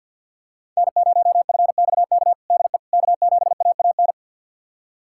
Morse code, heard as a keyed tone, about 51 wpm, 700 Hz. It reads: N0FXK DE K7AAN
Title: USA Callsign DE Callsign - 5 Characters - 50wpm